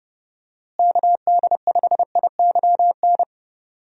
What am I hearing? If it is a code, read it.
Morse code, heard as a keyed tone, about 30 wpm, 700 Hz. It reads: KD5IYN
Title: USA Call Signs - 6 Characters - Repeated 3x - 30wpm